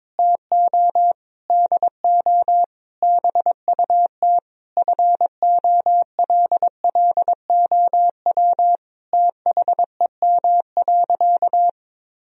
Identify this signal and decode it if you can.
Morse code, heard as a keyed tone, about 22 wpm, 700 Hz. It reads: TO DO BUT FOLLOW THEM.
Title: Sentences from Top 1000 Words - Review Part 1 - 22wpm